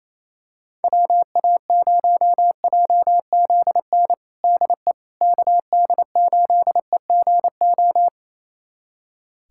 Morse code, heard as a keyed tone, about 27 wpm, 700 Hz. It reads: WA0JZN DE KD8EGO
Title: USA Callsign DE Callsign 28wpm